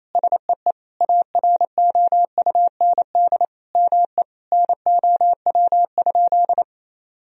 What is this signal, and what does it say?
Morse code, heard as a keyed tone, about 28 wpm, 700 Hz. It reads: SEE AROUND ME NOW?
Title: Sentences from Top 1000 Words - Review Part 3 - 28wpm